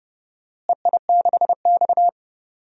Morse code, heard as a keyed tone, about 31 wpm, 700 Hz. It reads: EI6X
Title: DXCC-100 Call Signs - 4 Characters - Repeated 3x - 30wpm